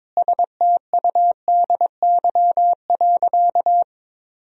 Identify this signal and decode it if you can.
Morse code, heard as a keyed tone, about 22 wpm, 700 Hz. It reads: STUDY.
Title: 3x Word-Spacing - Sentences from Top 300 Words - 22wpm